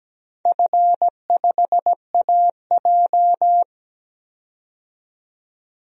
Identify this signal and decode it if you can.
Morse code, heard as a keyed tone, about 17 wpm, 700 Hz. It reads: F5AJ
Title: DXCC-100 Call Signs - 4 Characters - 17wpm